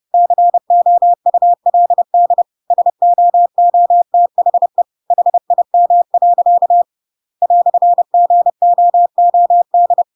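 Morse code, heard as a keyed tone, about 30 wpm, 700 Hz. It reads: COULD SOOTHE HIM. "GOOD
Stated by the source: Chapter 2 of The Adventures of Pinocchio read as a book in Morse code (CW) at 30 wpm, by ik1hge.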